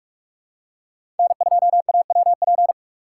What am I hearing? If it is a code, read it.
Morse code, heard as a keyed tone, about 45 wpm, 700 Hz. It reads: N2AWP
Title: USA Call Signs - 6x Character Spacing - 45wpm